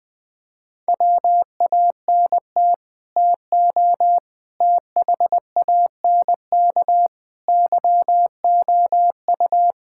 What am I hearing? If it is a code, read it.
Morse code, heard as a keyed tone, about 20 wpm, 700 Hz. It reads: WANT TO THANK YOU
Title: Sets of 8 Words - 20wpm